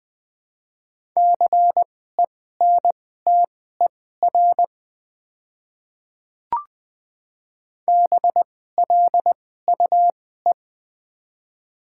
Morse code, heard as a keyed tone, about 21 wpm, spaced as for 15 wpm, 700 Hz. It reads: CENTER BLUE
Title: 2x Character Spacing - Top 500 Words - 20wpm